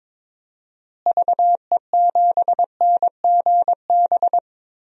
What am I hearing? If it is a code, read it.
Morse code, heard as a keyed tone, about 22 wpm, 700 Hz. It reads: VE7NGB
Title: DXCC-100 Call Signs - 6 Characters - Repeated 2x - 22wpm